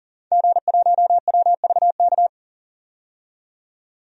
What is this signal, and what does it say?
Morse code, heard as a keyed tone, about 40 wpm, 700 Hz. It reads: G1WVK
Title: DXCC-100 Call Signs - 5 Characters - 40wpm